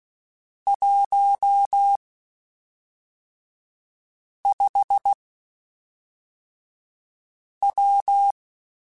Morse code, heard as a keyed tone, about 16 wpm, spaced as for 3 wpm, 800 Hz. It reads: 15W
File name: (spaced) / QSOs Spaced-mp3s / WD0BGP 1 @5 sp